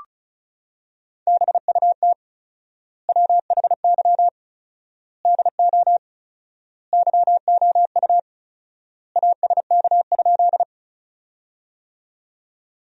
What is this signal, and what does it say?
Morse code, heard as a keyed tone, about 36 wpm, 700 Hz. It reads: BUT WHY DO YOU ASK?
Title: 4x Word-Spacing - Sentences from Top 200 Words - 35wpm